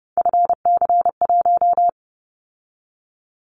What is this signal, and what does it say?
Morse code, heard as a keyed tone, about 30 wpm, 700 Hz. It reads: FC1
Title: Callsign Prefixes for Top 50 DXCC - 30wpm